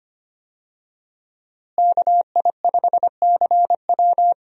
Morse code, heard as a keyed tone, about 25 wpm, 700 Hz. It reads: KI5CW
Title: USA Call Signs - 5 Characters - Repeated 2x - 25wpm